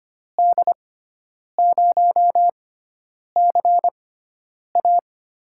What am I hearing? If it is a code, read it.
Morse code, heard as a keyed tone, about 25 wpm, spaced as for 9 wpm, 700 Hz. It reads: D0CA
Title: USA Call Signs - 6 Characters - 6x Character Spacing - 25wpm